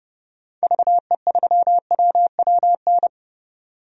Morse code, heard as a keyed tone, about 30 wpm, 700 Hz. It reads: VE3WWN
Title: DXCC-100 Call Signs - 6 Characters - 30wpm